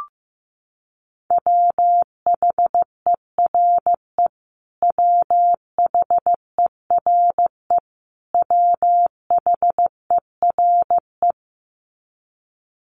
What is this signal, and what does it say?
Morse code, heard as a keyed tone, about 15 wpm, 700 Hz. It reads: WHERE WHERE WHERE
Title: Top 300 Words Repeated 15wpm